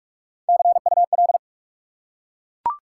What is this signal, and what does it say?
Morse code, heard as a keyed tone, about 44 wpm, 700 Hz. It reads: CUL
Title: Top 180 QSO Elements 45wpm